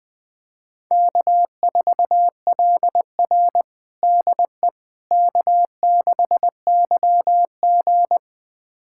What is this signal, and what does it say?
Morse code, heard as a keyed tone, about 20 wpm, 700 Hz. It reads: K4LR DE K6YG
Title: USA Callsign DE Callsign - 4 Characters - 20wpm